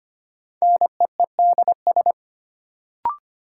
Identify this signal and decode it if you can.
Morse code, heard as a keyed tone, about 25 wpm, 700 Hz. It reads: NEEDS
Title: Sets of 3 Words - New Word Drill - 25wpm